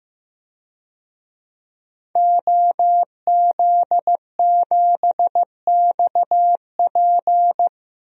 Morse code, heard as a keyed tone, about 15 wpm, 700 Hz. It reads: OZ7XP
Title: DXCC-100 Call Signs - 5 Characters - Repeated 2x - 15wpm